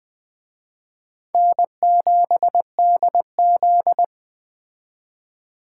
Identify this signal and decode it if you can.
Morse code, heard as a keyed tone, about 20 wpm, 700 Hz. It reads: N7DZ